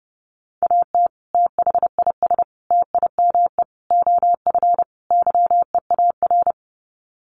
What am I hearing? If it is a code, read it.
Morse code, heard as a keyed tone, about 28 wpm, 700 Hz. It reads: AT THIS TIME OF YEAR
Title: Sets of 5 Words Encore2 30wpm